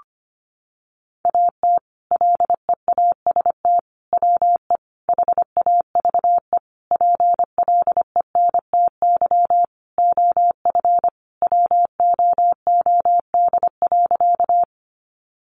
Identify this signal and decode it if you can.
Morse code, heard as a keyed tone, about 25 wpm, 700 Hz. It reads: AT LEAST WE HAVE PLENTY OF WOOD.
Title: Sentences from Top 1300 Words 25wpm